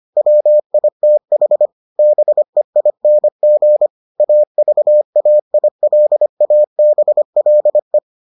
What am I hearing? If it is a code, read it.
Morse code, heard as a keyed tone, about 25 wpm, 600 Hz. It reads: WITH BEING AVAILABLE